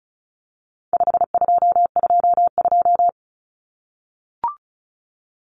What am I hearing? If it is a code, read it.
Morse code, heard as a keyed tone, about 34 wpm, 700 Hz. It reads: 5222
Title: Numbers - 4-Digit 35wpm